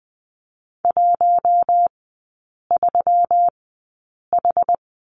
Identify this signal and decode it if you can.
Morse code, heard as a keyed tone, about 20 wpm, spaced as for 8 wpm, 700 Hz. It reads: 13H
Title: 205 Lesson 33e - Letter j - Three Characters 20wpm